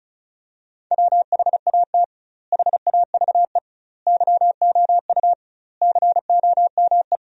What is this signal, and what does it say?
Morse code, heard as a keyed tone, about 36 wpm, 700 Hz. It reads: WHAT HAVE YOU COME
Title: 2x Word-Spacing - Sentences from Top 100 Words - 35wpm